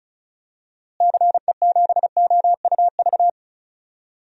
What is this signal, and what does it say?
Morse code, heard as a keyed tone, about 36 wpm, 700 Hz. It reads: CE7OUV
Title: DXCC-100 Call Signs - 6 Characters - Repeated 3x - 35wpm